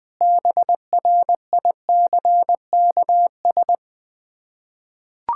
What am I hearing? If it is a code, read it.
Morse code, heard as a keyed tone, about 20 wpm, 700 Hz. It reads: BRICKS
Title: Difficult Words with Hints - 20wpm